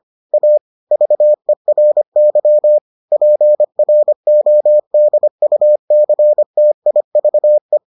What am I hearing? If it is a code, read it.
Morse code, heard as a keyed tone, about 25 wpm, 600 Hz. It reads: A VERY PRODUCTIVE